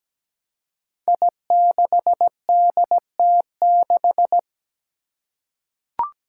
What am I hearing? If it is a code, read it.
Morse code, heard as a keyed tone, about 17 wpm, 700 Hz. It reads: I6DT6
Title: Five Challenging Letter-Number Combinations 17wpm